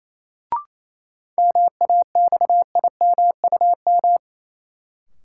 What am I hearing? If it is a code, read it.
Morse code, heard as a keyed tone, about 28 wpm, 700 Hz. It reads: MAXIMUM
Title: Sentences from Top 1300 Words - New Word Drill Alt - 28wpm